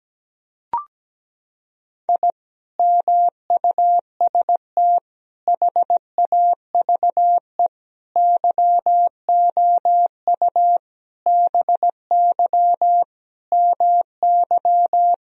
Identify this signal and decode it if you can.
Morse code, heard as a keyed tone, about 17 wpm, 700 Hz. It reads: I MUST HAVE YOU BY MY
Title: Sentences from Top 1000 Words - Review Part 1 - 17wpm